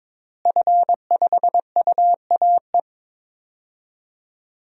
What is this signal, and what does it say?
Morse code, heard as a keyed tone, about 22 wpm, 700 Hz. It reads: F5UAE
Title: DXCC-100 Call Signs - 5 Characters - Repeated 3x - 22wpm